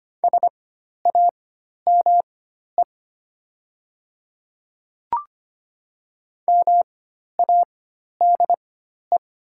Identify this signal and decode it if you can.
Morse code, heard as a keyed tone, about 25 wpm, spaced as for 12 wpm, 700 Hz. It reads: SAME MADE
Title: 4x Character Spacing - Top 200 Words - 25wpm